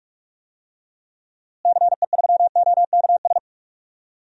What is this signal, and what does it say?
Morse code, heard as a keyed tone, about 42 wpm, 690 Hz. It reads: CE3OKS
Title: DXCC-100 Call Signs - 6 Characters - Repeated 2x - 45wpm